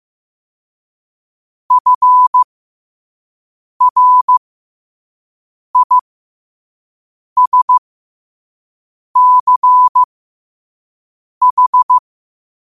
Morse code, heard as a keyed tone, about 15 wpm, spaced as for 5 wpm, 1000 Hz. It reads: FRISCH
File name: qotdfiles / 5wpm